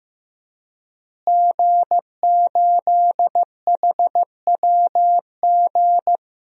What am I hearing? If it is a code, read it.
Morse code, heard as a keyed tone, about 16 wpm, 700 Hz. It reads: G8HWG
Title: DXCC-100 Call Signs - 5 Characters - Repeated 3x - 15wpm